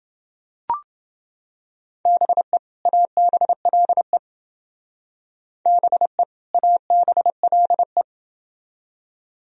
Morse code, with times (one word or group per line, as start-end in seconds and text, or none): 2.1-2.6 BE
2.9-4.2 ABLE
5.7-6.2 BE
6.5-8.0 ABLE